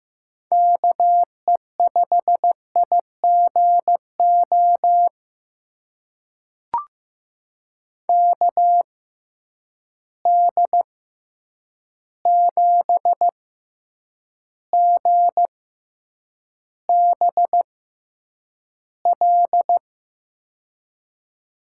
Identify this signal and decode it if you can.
Morse code, heard as a keyed tone, about 15 wpm, spaced as for 8 wpm, 700 Hz. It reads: KE5IGO KD7GBL
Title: USA Call Signs - 6 Characters - 6x Character Spacing - 15wpm